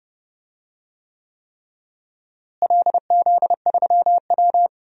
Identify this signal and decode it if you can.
Morse code, heard as a keyed tone, about 29 wpm, 700 Hz. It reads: LZ3W
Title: DXCC-100 Call Signs - 4 Characters - Repeated 3x - 30wpm